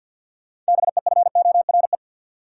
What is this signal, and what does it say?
Morse code, heard as a keyed tone, about 53 wpm, 700 Hz. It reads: BEFORE
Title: Sentences from Top 200 Words - New Word Drill - 50wpm